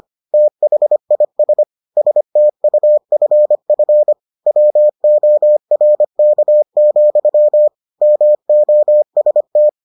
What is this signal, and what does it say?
Morse code, heard as a keyed tone, about 25 wpm, 600 Hz. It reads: THIS STUFF WORK, MOST